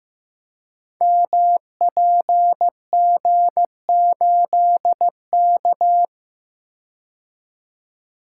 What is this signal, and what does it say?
Morse code, heard as a keyed tone, about 15 wpm, 700 Hz. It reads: MPG8K
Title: Code Groups - 5 Letter:Numbers - 15wpm